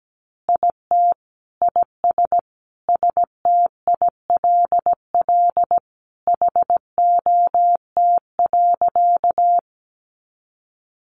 Morse code, heard as a keyed tone, about 17 wpm, 700 Hz. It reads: IT IS STILL HOT.
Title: Easy Phrases 17wpm